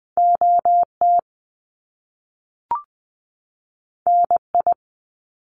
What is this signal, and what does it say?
Morse code, heard as a keyed tone, about 20 wpm, 700 Hz. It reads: O T N I
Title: Top Two Letter Combinations in Words 20wpm